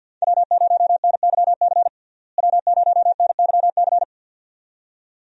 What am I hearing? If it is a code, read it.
Morse code, heard as a keyed tone, about 50 wpm, 700 Hz. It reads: W0NYC W0NYC
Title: USA Call Signs - 5 Characters - Repeated 2x - 50wpm